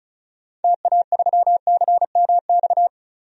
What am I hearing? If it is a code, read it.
Morse code, heard as a keyed tone, about 36 wpm, 700 Hz. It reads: TA3CMX
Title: DXCC-100 Call Signs - 6 Characters - 35wpm